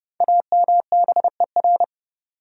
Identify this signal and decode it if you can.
Morse code, heard as a keyed tone, about 29 wpm, 700 Hz. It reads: AMBER
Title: USA Common Names - Max 5 Letters - 30wpm